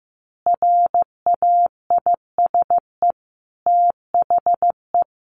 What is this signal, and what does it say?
Morse code, heard as a keyed tone, about 15 wpm, 700 Hz. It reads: RAISE THE
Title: A Sound Of Thunder 15wpm